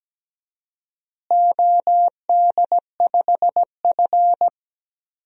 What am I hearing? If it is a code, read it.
Morse code, heard as a keyed tone, about 17 wpm, 700 Hz. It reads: OD5F
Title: DXCC-100 Call Signs - 4 Characters - Repeated 3x - 17wpm